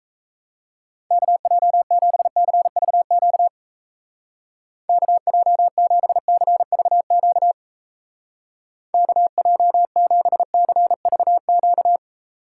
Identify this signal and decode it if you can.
Morse code, heard as a keyed tone, about 36 wpm, 700 Hz. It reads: KJ7CVQ KJ7CVQ KJ7CVQ